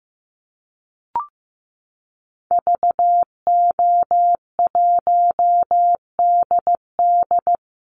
Morse code, heard as a keyed tone, about 15 wpm, 700 Hz. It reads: VO1DD
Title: Call Signs Repeated 3x - 15wpm